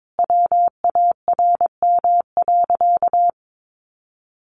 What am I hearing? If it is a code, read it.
Morse code, heard as a keyed tone, about 22 wpm, 700 Hz. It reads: WARM.
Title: Sentences from Top 1500 Words 22wpm